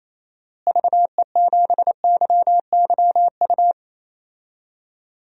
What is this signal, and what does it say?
Morse code, heard as a keyed tone, about 28 wpm, 700 Hz. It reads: VE7YYU
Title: DXCC-100 Call Signs - 6 Characters - Repeated 2x - 28wpm